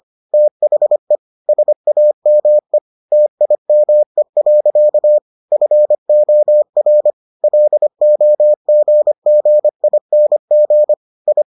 Morse code, heard as a keyed tone, about 25 wpm, 600 Hz. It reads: THE SAME TIME. FOR LOGGING I